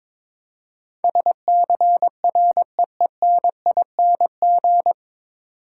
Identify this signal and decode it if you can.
Morse code, heard as a keyed tone, about 22 wpm, 700 Hz. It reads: SCREENING
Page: Morse Code Ninja practice 22wpm